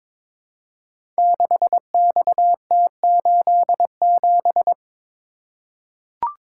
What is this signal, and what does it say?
Morse code, heard as a keyed tone, about 22 wpm, 700 Hz. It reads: 6XT87
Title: Code Groups - 5 Letter:Numbers - 22wpm